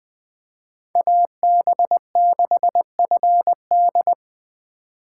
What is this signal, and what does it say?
Morse code, heard as a keyed tone, about 20 wpm, 700 Hz. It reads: AB6FD